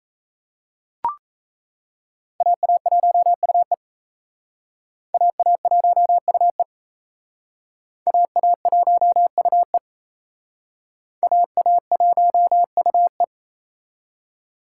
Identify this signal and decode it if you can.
Morse code, heard as a keyed tone, about 35 wpm, 700 Hz. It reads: AA1UE AA1UE AA1UE AA1UE
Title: Speed Racing - USA 2x2 Call Signs 28wpm